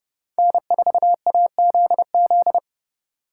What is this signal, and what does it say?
Morse code, heard as a keyed tone, about 28 wpm, 700 Hz. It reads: N4AZZ